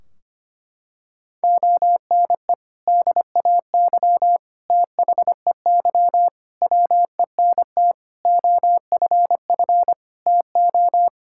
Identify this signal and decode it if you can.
Morse code, heard as a keyed tone, about 25 wpm, 700 Hz. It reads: ONE DAY THEY WENT OFF TO
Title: Sentences from Top 1600 Words 25wpm